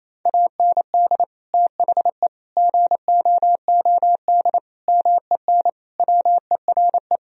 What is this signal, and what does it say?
Morse code, heard as a keyed tone, about 28 wpm, 700 Hz. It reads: AND THE GOOD MEN WERE
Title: Sentences from Top 1000 Words - Review Part 2 - 28wpm